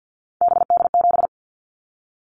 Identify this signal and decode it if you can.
Morse code, heard as a keyed tone, about 50 wpm, 700 Hz. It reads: BD7
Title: Callsign Prefixes for Top 100 DXCC - 50wpm